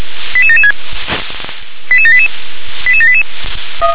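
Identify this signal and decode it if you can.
Five-tone selective call (ZVEI), heard as quick runs of five tones, tones of about 70 ms. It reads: 88776 88788 88788
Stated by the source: ZVEI five-tone selective call received by SDR.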